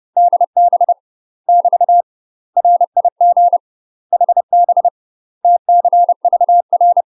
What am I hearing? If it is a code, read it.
Morse code, heard as a keyed tone, about 30 wpm, 700 Hz. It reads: DB <BT> RIG HB TCVR
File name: UK MASTER QSOs TRAINING / 30WPM / UK MASTER QSOs 0027